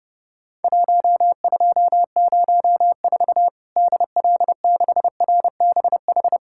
Morse code, heard as a keyed tone, about 30 wpm, 700 Hz. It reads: 1204 DL6RBH
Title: Worked All Europe DX Contest - QTC Exchange - 30wpm